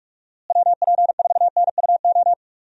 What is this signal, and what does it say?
Morse code, heard as a keyed tone, about 46 wpm, 700 Hz. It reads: WP4NUO
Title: USA Call Signs - 6 Characters - Repeated 3x - 45wpm